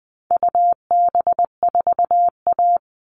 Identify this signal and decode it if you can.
Morse code, heard as a keyed tone, about 20 wpm, 700 Hz. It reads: UB4A
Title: All Cut Numbers - 4-Digit 20wpm